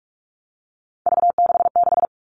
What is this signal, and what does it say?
Morse code, heard as a keyed tone, about 44 wpm, 700 Hz. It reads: V66
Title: All Cut Numbers - 3-Digit 45wpm